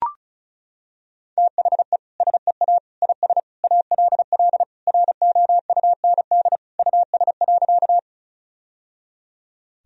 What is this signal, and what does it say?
Morse code, heard as a keyed tone, about 34 wpm, 700 Hz. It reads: THE SEA IS ALL ROUND US.
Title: Sentences from Top 1000 Words - Review Part 1 - 35wpm